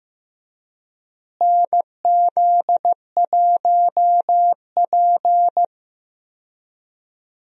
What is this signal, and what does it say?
Morse code, heard as a keyed tone, about 15 wpm, 700 Hz. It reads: NZ1P
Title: USA Call Signs - 4 Characters - Repeated 3x - 15wpm